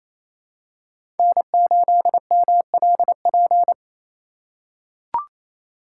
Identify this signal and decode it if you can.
Morse code, heard as a keyed tone, about 28 wpm, 700 Hz. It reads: N8MLP
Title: USA Call Signs - 3x Character Spacing - 28wpm